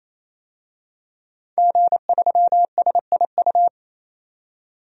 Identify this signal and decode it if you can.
Morse code, heard as a keyed tone, about 27 wpm, 700 Hz. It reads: G3SIU
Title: DXCC-100 Call Signs - 5 Characters - 28wpm